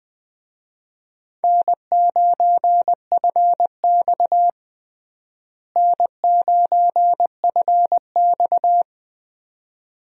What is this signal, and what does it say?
Morse code, heard as a keyed tone, about 21 wpm, 700 Hz. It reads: N9FX N9FX
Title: USA Call Signs - 4 Characters - Repeated 2x - 20wpm